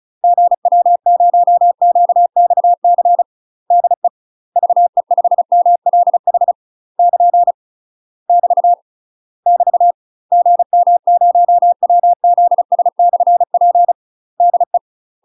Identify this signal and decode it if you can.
Morse code, heard as a keyed tone, about 33 wpm, 700 Hz. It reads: GW0QXC DE VE5MLH <KN> <BT> <BT> GM0WZS/P DE
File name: UK MASTER QSOs TRAINING/35WPM